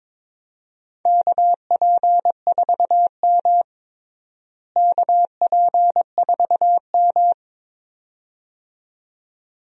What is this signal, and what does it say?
Morse code, heard as a keyed tone, about 22 wpm, 700 Hz. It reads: KP4M KP4M
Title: DXCC-100 Call Signs - 4 Characters - Repeated 2x - 22wpm